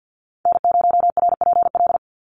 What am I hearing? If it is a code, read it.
Morse code, heard as a keyed tone, about 50 wpm, 700 Hz. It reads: N0RPL
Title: USA Call Signs 50wpm